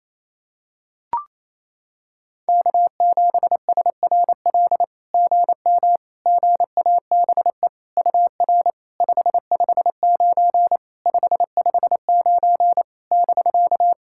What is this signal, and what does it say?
Morse code, heard as a keyed tone, about 28 wpm, 700 Hz. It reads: K7SRL GM GABE UR 559 559 <BK>
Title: SOTA QSO Practice 28wpm